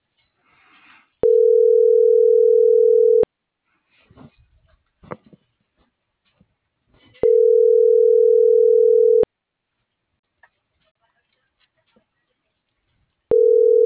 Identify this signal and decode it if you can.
Ringback tone (North America), 440 + 480 Hz.